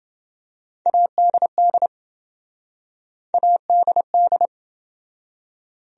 Morse code, heard as a keyed tone, about 28 wpm, 700 Hz. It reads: ADD ADD